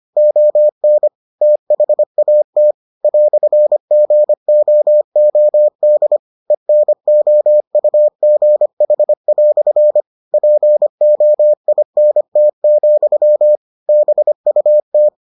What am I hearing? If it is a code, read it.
Morse code, heard as a keyed tone, about 25 wpm, 600 Hz. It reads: ON THAT "GOOD ENOUGH" POINT, BUT